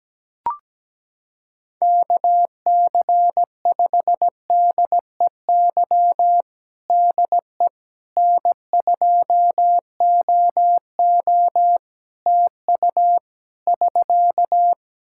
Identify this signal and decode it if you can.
Morse code, heard as a keyed tone, about 17 wpm, 700 Hz. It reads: KC5DEY DE N2OO TU <SK>
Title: K1USN SST QSO Practice 17wpm